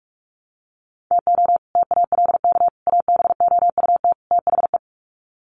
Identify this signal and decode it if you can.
Morse code, heard as a keyed tone, about 44 wpm, 700 Hz. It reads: TO TALK ABOUT THE
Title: Sets of 4 Words 45wpm